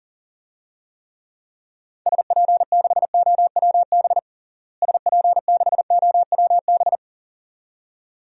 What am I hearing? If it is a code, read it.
Morse code, heard as a keyed tone, about 40 wpm, 700 Hz. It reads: SP6OWB SP6OWB
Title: DXCC-100 Call Signs - 6 Characters - Repeated 2x - 40wpm